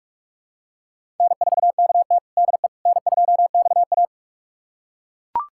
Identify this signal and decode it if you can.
Morse code, heard as a keyed tone, about 42 wpm, 700 Hz. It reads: N4KT DE N2XA
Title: USA Callsign DE Callsign - 4 Characters - 45wpm